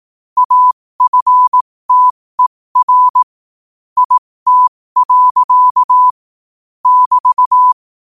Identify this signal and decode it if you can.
Morse code, heard as a keyed tone, about 18 wpm, spaced as for 15 wpm, 1000 Hz. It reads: AFTER IT. <BT>
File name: 15wpm